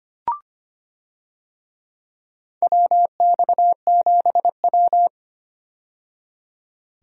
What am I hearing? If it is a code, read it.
Morse code, heard as a keyed tone, about 25 wpm, 700 Hz. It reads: WX7W
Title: USA Call Signs - 4 Characters - 25wpm